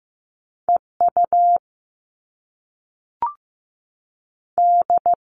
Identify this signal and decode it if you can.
Morse code, heard as a keyed tone, about 15 wpm, 700 Hz. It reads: E U D
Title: All Cut Numbers - 2-Digit 15wpm